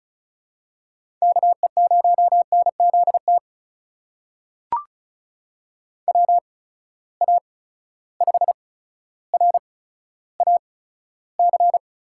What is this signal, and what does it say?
Morse code, heard as a keyed tone, about 36 wpm, spaced as for 15 wpm, 700 Hz. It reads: KE0NZT WA5RAC
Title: USA Call Signs - 6 Characters - 8x Character Spacing - 35wpm